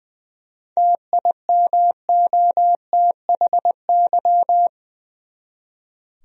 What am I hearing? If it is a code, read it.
Morse code, heard as a keyed tone, about 20 wpm, 700 Hz. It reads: TIMOTHY
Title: USA Common Names - 20wpm